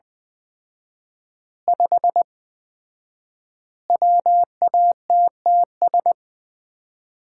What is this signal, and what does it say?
Morse code, heard as a keyed tone, about 20 wpm, 700 Hz. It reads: 5 WATTS
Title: Common QSO Phrases - 4x Word-Spacing - 20wpm